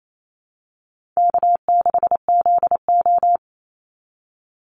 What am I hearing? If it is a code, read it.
Morse code, heard as a keyed tone, about 28 wpm, 700 Hz. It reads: K6ZO
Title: Easy Call Signs - 28wpm